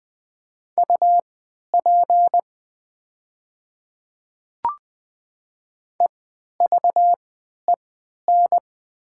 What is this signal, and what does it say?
Morse code, heard as a keyed tone, about 20 wpm, spaced as for 11 wpm, 700 Hz. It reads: UP EVEN
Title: 3x Character Spacing - Top 500 Words - 20wpm